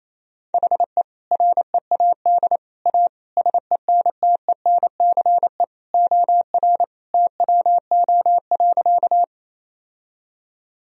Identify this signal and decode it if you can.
Morse code, heard as a keyed tone, about 28 wpm, 700 Hz. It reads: HE READ A SENTENCE OR TWO.
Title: Sentences from Top 1000 Words - Review Part 1 - 28wpm